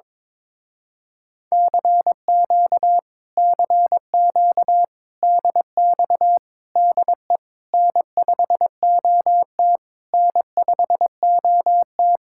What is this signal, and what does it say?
Morse code, heard as a keyed tone, about 22 wpm, 700 Hz. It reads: CQ CQ DX DE N5OT N5OT
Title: Speed-Racing Common QSO Phrases - 22wpm